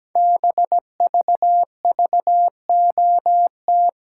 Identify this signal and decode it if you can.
Morse code, heard as a keyed tone, about 17 wpm, 700 Hz. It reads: BVVOT
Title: Code Groups - 5 Letter:Numbers - 17wpm